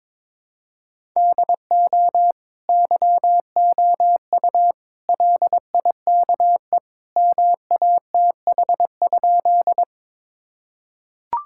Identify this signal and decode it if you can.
Morse code, heard as a keyed tone, about 22 wpm, 700 Hz. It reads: DO YOU LIKE MATH?
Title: Sentences from Top 1000 Words - Review Part 6 - 22wpm